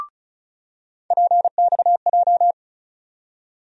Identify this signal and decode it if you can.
Morse code, heard as a keyed tone, about 36 wpm, 700 Hz. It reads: PXJ
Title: USA Call Sign Postfixes - 3 Letters 35wpm